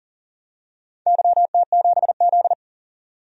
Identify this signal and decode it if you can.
Morse code, heard as a keyed tone, about 38 wpm, 700 Hz. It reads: YT7Z
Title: DXCC-100 Call Signs - 4 Characters - 40wpm